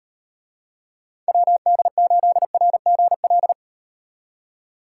Morse code, heard as a keyed tone, about 38 wpm, 700 Hz. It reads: WD8RGL